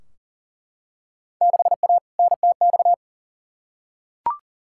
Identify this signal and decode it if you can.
Morse code, heard as a keyed tone, about 40 wpm, 700 Hz. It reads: 6A NTX
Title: Field Day Exchanges - 40wpm